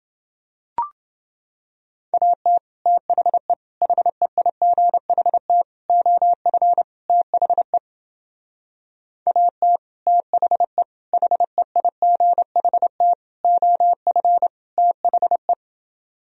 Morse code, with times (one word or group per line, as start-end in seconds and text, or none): 2.1-2.6 AT
2.9-3.5 THE
3.8-5.6 HEIGHT
5.9-6.8 OF
7.1-7.8 THE
9.3-9.8 AT
10.1-10.8 THE
11.1-13.1 HEIGHT
13.4-14.5 OF
14.8-15.5 THE